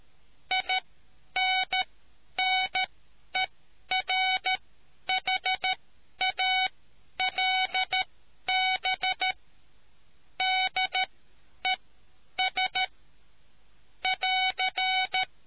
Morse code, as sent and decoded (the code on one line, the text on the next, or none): .. -. -. .-. .... .- .-.. -... / -.. ... / .-.-.
INNERHALB DES <AR>